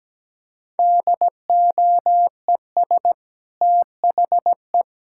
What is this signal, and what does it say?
Morse code, heard as a keyed tone, about 17 wpm, 700 Hz. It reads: DOES THE
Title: Sentences from Top 1000 Words - Review Part 5 - 17wpm